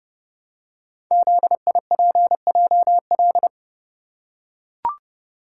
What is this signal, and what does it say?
Morse code, heard as a keyed tone, about 30 wpm, 700 Hz. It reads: ZIPJL
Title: Code Groups - 5 Letter:Numbers - 30wpm